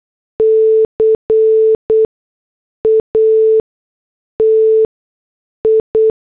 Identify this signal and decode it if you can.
Morse code, heard as a keyed tone, about 8 wpm, spaced as for 6 wpm, 440 Hz. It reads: CATI